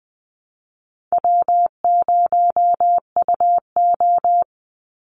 Morse code, heard as a keyed tone, about 20 wpm, 700 Hz. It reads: W0UO
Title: Easy Call Signs Repeated 2x - 20wpm